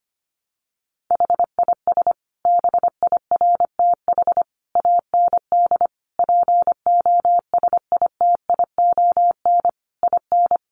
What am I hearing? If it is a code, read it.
Morse code, heard as a keyed tone, about 25 wpm, 700 Hz. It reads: HIS BIRTH AND POSITION IN
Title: Sentences from Top 900 Words 25wpm